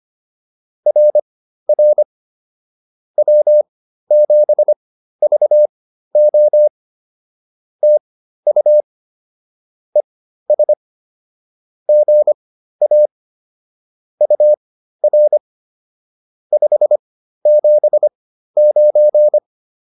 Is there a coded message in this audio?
Yes: Morse code from 0.9 to 19.4 s, heard as a keyed tone, about 25 wpm, spaced as for 13 wpm, 600 Hz. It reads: RR W7VO TU ES GA UR 579